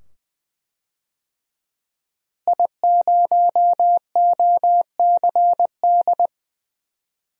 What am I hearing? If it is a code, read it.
Morse code, heard as a keyed tone, about 21 wpm, 700 Hz. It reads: I0OCD
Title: DXCC-100 Call Signs - 5 Characters - Repeated 2x - 20wpm